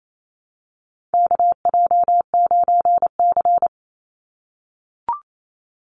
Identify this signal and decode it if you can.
Morse code, heard as a keyed tone, about 28 wpm, 700 Hz. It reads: KJ9C
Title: Call Signs - 28wpm